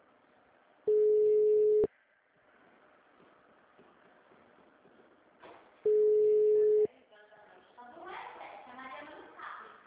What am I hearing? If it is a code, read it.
Ringback tone (Europe, 425 Hz), with speech.